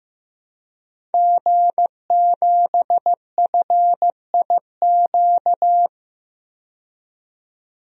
Morse code, heard as a keyed tone, about 15 wpm, 700 Hz. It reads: G7FIQ
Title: DXCC-100 Call Signs - 5 Characters - 15wpm